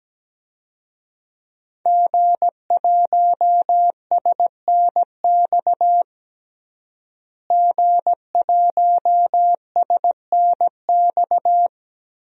Morse code, heard as a keyed tone, about 17 wpm, 700 Hz. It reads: G1SNX G1SNX
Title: DXCC-100 Call Signs - 5 Characters - Repeated 3x - 17wpm